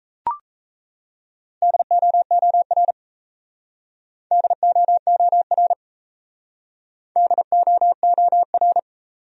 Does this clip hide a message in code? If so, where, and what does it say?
Yes: Morse code from 1.6 to 8.8 s, heard as a keyed tone, about 35 wpm, 700 Hz. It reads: DOOR DOOR DOOR